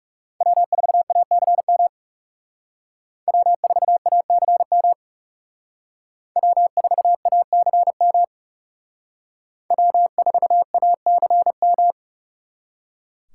Morse code, heard as a keyed tone, about 36 wpm, 700 Hz. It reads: W4ACM W4ACM W4ACM W4ACM
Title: Speed Racing - USA 1x3 Call Signs 30wpm